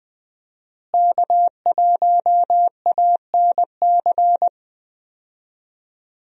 Morse code, heard as a keyed tone, about 20 wpm, 700 Hz. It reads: K1ANC
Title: USA Call Signs - 5 Characters - Repeated 2x - 20wpm